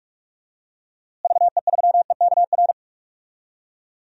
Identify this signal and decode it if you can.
Morse code, heard as a keyed tone, about 46 wpm, 700 Hz. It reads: VE3EKR
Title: DXCC-100 Call Signs - 6 Characters - Repeated 3x - 45wpm